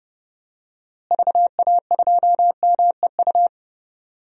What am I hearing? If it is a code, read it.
Morse code, heard as a keyed tone, about 32 wpm, 700 Hz. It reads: VA2MEU
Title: DXCC-100 Call Signs - 6 Characters - Repeated 3x - 30wpm